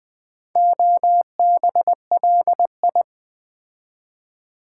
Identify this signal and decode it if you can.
Morse code, heard as a keyed tone, about 20 wpm, 700 Hz. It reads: OBLI
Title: Build A Word - 4 Letters At A Time - 10-Letter Words - 20wpm